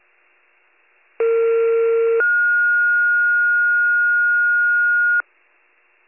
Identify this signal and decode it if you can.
Two-tone page, heard as a one-second tone, then a three-second tone, about 460 then 1430 Hz. It is heard from 1.2 to 5.2 s.